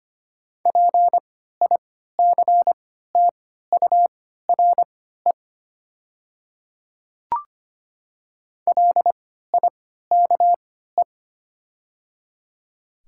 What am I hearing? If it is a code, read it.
Morse code, heard as a keyed tone, about 25 wpm, spaced as for 14 wpm, 690 Hz. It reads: PICTURE LIKE
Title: 3x Character Spacing - Top 200 Words - 25wpm